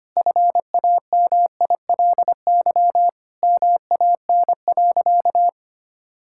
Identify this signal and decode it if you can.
Morse code, heard as a keyed tone, about 25 wpm, 700 Hz. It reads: FAMILY MAN.
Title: Sentences from Top 1000 Words - Review Part 5 - 25wpm